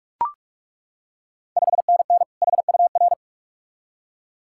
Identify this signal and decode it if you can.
Morse code, heard as a keyed tone, about 41 wpm, 700 Hz. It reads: 5NN HUR
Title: Ontario QSO Party Exchange 45wpm